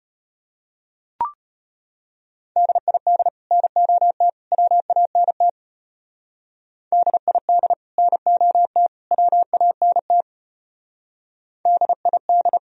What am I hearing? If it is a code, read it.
Morse code, heard as a keyed tone, about 32 wpm, 700 Hz. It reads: DID NOT WANT DID NOT WANT DID